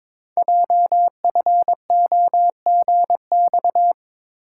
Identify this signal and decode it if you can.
Morse code, heard as a keyed tone, about 22 wpm, 700 Hz. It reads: JFOGX
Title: Code Groups - 5 Letters - 22wpm